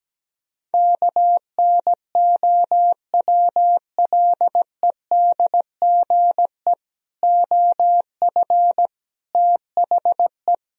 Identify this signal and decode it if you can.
Morse code, heard as a keyed tone, about 17 wpm, 700 Hz. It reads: KNOWLEDGE OF THE
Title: Sentences from Top 1000 Words - Review Part 3 - 17wpm